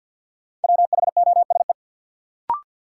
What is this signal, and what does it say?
Morse code, heard as a keyed tone, about 53 wpm, 700 Hz. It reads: WHOSE
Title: Sentences from Top 1200 Words - New Word Drill Alt - 50wpm